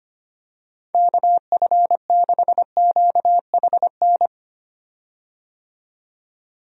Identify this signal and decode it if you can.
Morse code, heard as a keyed tone, about 25 wpm, 700 Hz. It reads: KF6QHN
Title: USA Call Signs - 6 Characters - 25wpm